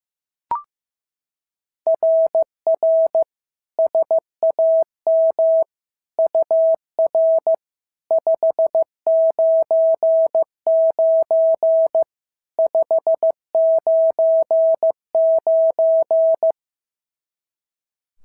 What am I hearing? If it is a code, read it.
Morse code, heard as a keyed tone, about 14 wpm, 650 Hz. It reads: RR SAM UR 599 599
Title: SOTA QSO Practice 15wpm